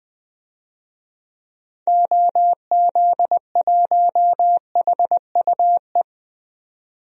Morse code, heard as a keyed tone, about 20 wpm, 700 Hz. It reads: OZ1HUE
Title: DXCC-100 Call Signs - 6 Characters - 20wpm